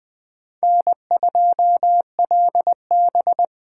Morse code, heard as a keyed tone, about 20 wpm, 700 Hz. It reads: N2LB
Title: USA Call Signs - 8x Character Spacing - 20wpm